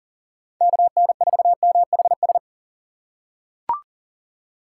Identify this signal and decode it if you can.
Morse code, heard as a keyed tone, about 40 wpm, 700 Hz. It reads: KN4MHS
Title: USA Call Signs - 6 Characters - 4x Character Spacing - 40wpm